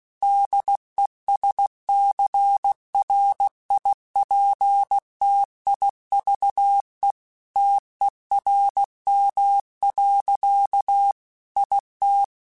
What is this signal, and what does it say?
Morse code, heard as a keyed tone, about 16 wpm, 800 Hz. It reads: DESCRIPTIVE TERM. IT
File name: Technical Subjects / Technical Subjects-mp3s / Reactance @15